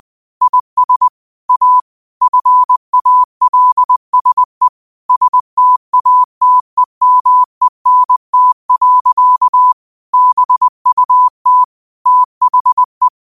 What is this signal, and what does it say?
Morse code, heard as a keyed tone, about 20 wpm, 1000 Hz. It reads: IS A FALSE STATEMENT. BUT THE
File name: qotdfiles / 20wpm